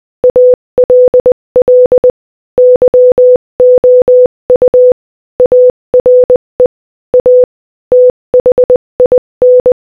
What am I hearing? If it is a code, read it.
Morse code, heard as a keyed tone, about 20 wpm, 500 Hz. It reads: ALL YOU ARE A THIN
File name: Morse-Code-Audio-Speed-20-7